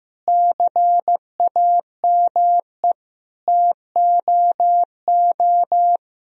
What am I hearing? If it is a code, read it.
Morse code, heard as a keyed tone, about 15 wpm, 700 Hz. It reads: CAME TOO
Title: Sentences from Top 1000 Words - Review Part 4 - 15wpm